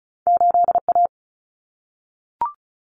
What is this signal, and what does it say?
Morse code, heard as a keyed tone, about 36 wpm, 700 Hz. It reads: 8A
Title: Common Cut Numbers - 2-Digit 35wpm